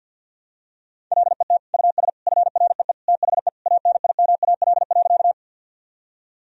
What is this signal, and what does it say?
Morse code, heard as a keyed tone, about 53 wpm, 700 Hz. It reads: LET US FREE THE ANIMAL.
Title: Sentences from Top 1000 Words - Review Part 1 - 50wpm